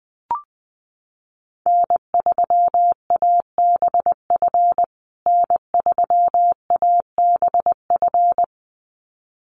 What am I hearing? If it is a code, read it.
Morse code, heard as a keyed tone, about 20 wpm, 700 Hz. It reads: N3ABF N3ABF
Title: USA Call Signs Repeated 2x 20wpm